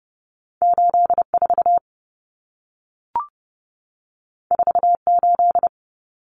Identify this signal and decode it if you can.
Morse code, heard as a keyed tone, about 30 wpm, 700 Hz. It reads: 84 48
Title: Common Cut Numbers - 2-Digit 30wpm